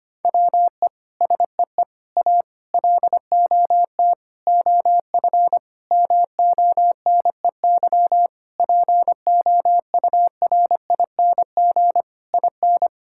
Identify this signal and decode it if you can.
Morse code, heard as a keyed tone, about 25 wpm, 700 Hz. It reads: WE SEE A LOT OF MONEY POURING IN
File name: Sets of 9 Words - 25wpm